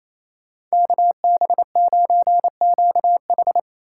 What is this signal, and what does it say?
Morse code, heard as a keyed tone, about 34 wpm, 700 Hz. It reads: KB9QH
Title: Speed Racing - USA 2x2 Call Signs 28wpm